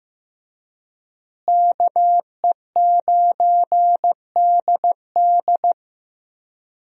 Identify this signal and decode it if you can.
Morse code, heard as a keyed tone, about 15 wpm, 700 Hz. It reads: KE9DD
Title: USA Call Signs - 5 Characters - Repeated 3x - 15wpm